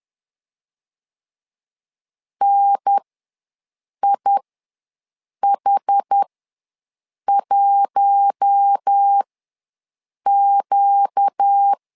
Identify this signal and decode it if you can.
Morse code, heard as a keyed tone, about 10 wpm, spaced as for 6 wpm, 800 Hz. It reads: NIH1Q
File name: LEZIONI / 4 Q1 ws 10wpm 2538